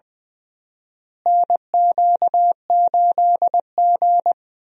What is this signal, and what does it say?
Morse code, heard as a keyed tone, about 20 wpm, 700 Hz. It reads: NQ8G
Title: USA Call Signs - 4 Characters - Repeated 2x - 20wpm